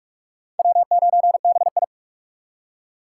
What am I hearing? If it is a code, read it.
Morse code, heard as a keyed tone, about 41 wpm, 700 Hz. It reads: W9BI
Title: USA Call Signs - 4 Characters - 4x Character Spacing - 45wpm